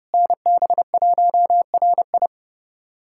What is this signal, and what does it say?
Morse code, heard as a keyed tone, about 32 wpm, 700 Hz. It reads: NB1RI
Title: USA Callsign DE Callsign 30wpm